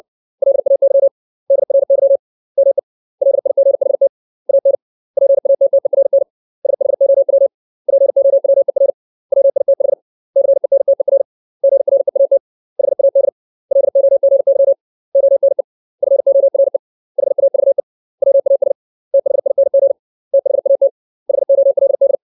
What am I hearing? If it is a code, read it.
Morse code, heard as a keyed tone, about 60 wpm, 550 Hz. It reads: LAY DRY ME LIGHT AN PATTERN SHOW POWER WITH CENTER GREAT HAD LOGY ONE FORE HAVE WAS THING THAT HOLD
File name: 632 words / 632 Words 60WPM